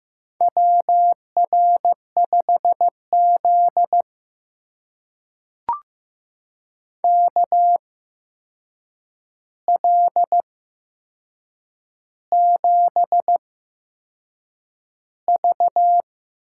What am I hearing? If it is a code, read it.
Morse code, heard as a keyed tone, about 15 wpm, spaced as for 6 wpm, 700 Hz. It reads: WR5Z KL7V
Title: USA Call Signs - 8x Character Spacing - 15wpm